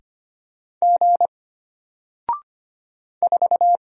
Morse code, heard as a keyed tone, about 25 wpm, 700 Hz. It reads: G 4
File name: Single-Letter-Number-25wpm